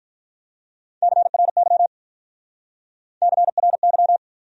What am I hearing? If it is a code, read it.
Morse code, heard as a keyed tone, about 45 wpm, 700 Hz. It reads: CRY CRY